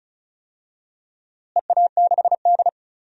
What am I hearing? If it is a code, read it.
Morse code, heard as a keyed tone, about 36 wpm, 700 Hz. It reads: EA6D
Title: DXCC-100 Call Signs - 4 Characters - Repeated 3x - 35wpm